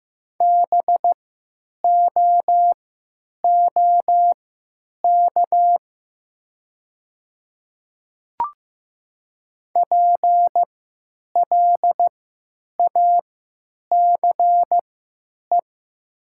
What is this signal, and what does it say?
Morse code, heard as a keyed tone, about 15 wpm, spaced as for 9 wpm, 700 Hz. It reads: BOOK PLACE
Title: 3x Character Spacing - Top 500 Words - 15wpm